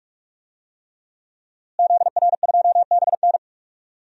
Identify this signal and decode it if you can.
Morse code, heard as a keyed tone, about 43 wpm, 700 Hz. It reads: ZR2BN
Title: DXCC-100 Call Signs - 5 Characters - Repeated 3x - 45wpm